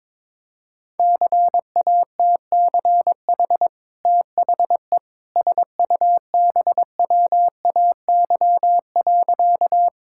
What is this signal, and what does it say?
Morse code, heard as a keyed tone, about 22 wpm, 700 Hz. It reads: CATCH THE SUBWAY.
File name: practice / 22wpm / Easy Sentences with Hints - 22wpm